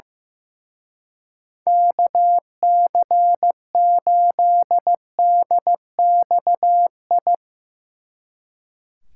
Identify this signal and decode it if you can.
Morse code, heard as a keyed tone, about 16 wpm, 700 Hz. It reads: KC8DXI
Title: Speed Racing - USA 2x3 Call Signs 15wpm